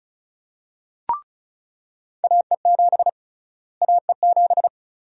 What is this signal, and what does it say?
Morse code, heard as a keyed tone, about 34 wpm, 700 Hz. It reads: AE7 AE7
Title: USA Call Sign Prefixes - Repeated 2x - 35wpm